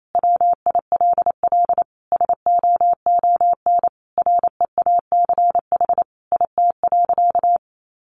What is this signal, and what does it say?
Morse code, heard as a keyed tone, about 28 wpm, 700 Hz. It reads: WILL SOON REACH IT.
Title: Sentences from Top 400 Words 28wpm